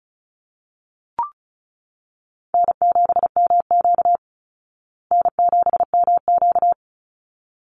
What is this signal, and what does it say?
Morse code, heard as a keyed tone, about 34 wpm, 700 Hz. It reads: N7MQ N7MQ
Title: Call Signs Repeated 3x - 35wpm